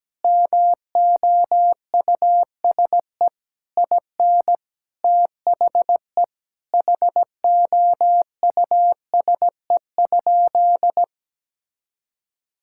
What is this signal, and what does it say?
Morse code, heard as a keyed tone, about 17 wpm, 700 Hz. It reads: MOUSE IN THE HOUSE?
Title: Sentences from Top 1000 Words - Review Part 5 - 17wpm